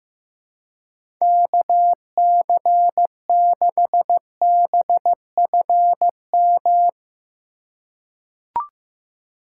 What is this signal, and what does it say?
Morse code, heard as a keyed tone, about 16 wpm, 700 Hz. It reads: KC6BFM
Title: USA Call Signs - 6 Characters - 4x Character Spacing - 15wpm